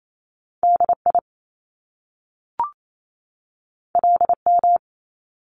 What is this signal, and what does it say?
Morse code, heard as a keyed tone, about 28 wpm, 700 Hz. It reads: DI LM
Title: Top Two Letter Combinations in Words 28wpm